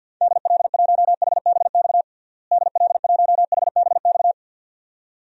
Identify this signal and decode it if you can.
Morse code, heard as a keyed tone, about 52 wpm, 700 Hz. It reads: DL1HBX DL1HBX
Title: DXCC-100 Call Signs - 6 Characters - Repeated 2x - 50wpm